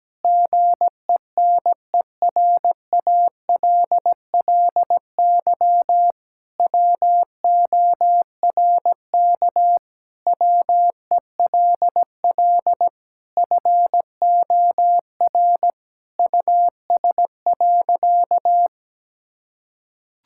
Morse code, heard as a keyed tone, about 17 wpm, 700 Hz. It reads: GENERALLY WORK WELL FOR US.
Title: Sentences from Top 1000 Words - Review Part 6 - 17wpm